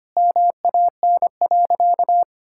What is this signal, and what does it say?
Morse code, heard as a keyed tone, about 25 wpm, 700 Hz. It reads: MAN.
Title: Sentences from Top 1000 Words - Review Part 6 - 25wpm